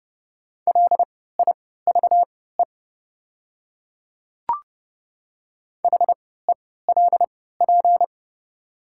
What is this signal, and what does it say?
Morse code, heard as a keyed tone, about 31 wpm, spaced as for 17 wpm, 700 Hz. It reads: LIVE HELP
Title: 3x Character Spacing - Top 200 Words - 30wpm